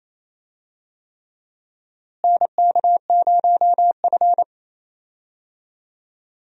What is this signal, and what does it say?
Morse code, heard as a keyed tone, about 28 wpm, 700 Hz. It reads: NK0F
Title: USA Call Signs - 4 Characters - Repeated 3x - 28wpm